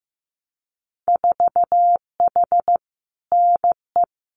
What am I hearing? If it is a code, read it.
Morse code, heard as a keyed tone, about 15 wpm, 700 Hz. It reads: 4H NE
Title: Winter Field Day Exchanges Encore 15wpm